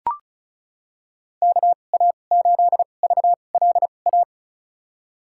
Morse code, heard as a keyed tone, about 34 wpm, spaced as for 25 wpm, 700 Hz. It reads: KA8VLA
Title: USA Call Signs - 2x Character Spacing - 35wpm